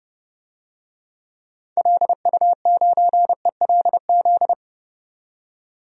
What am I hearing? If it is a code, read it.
Morse code, heard as a keyed tone, about 30 wpm, 700 Hz. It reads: LU9ELZ
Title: DXCC-100 Call Signs - 6 Characters - 30wpm